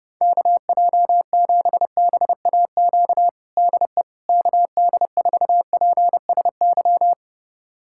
Morse code, heard as a keyed tone, about 30 wpm, 700 Hz. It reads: KJ7BAQ DE KD4PSY
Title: USA Callsign DE Callsign 30wpm